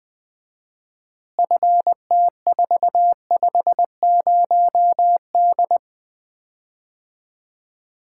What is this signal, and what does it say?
Morse code, heard as a keyed tone, about 20 wpm, 700 Hz. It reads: FT450D
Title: Common QSO Phrases - 4x Word-Spacing - 20wpm